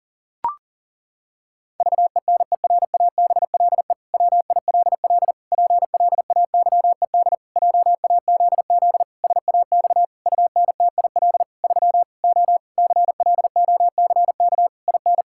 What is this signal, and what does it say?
Morse code, heard as a keyed tone, about 40 wpm, 700 Hz. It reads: VENERABLE WILL PLAYED JAZZ SAX UNTIL 3 O CLOCK IN
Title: Pangram Challenge v2 - 40wpm